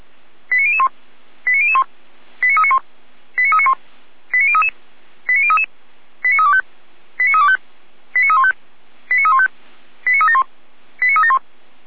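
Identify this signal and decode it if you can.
Five-tone selective call (ZVEI), heard as quick runs of five tones, tones of about 70 ms. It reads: 89001 89001 89391 89391 89033 89033 89325 89325 89315 89315 89371 89371
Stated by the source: ZVEI five-tone call received on SDR, channel 455, logged by the Feuerwehr Orlamünde fire brigade server.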